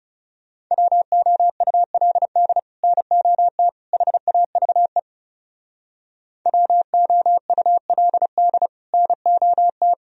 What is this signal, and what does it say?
Morse code, heard as a keyed tone, about 38 wpm, 700 Hz. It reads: WOULD NOT HAVE WOULD NOT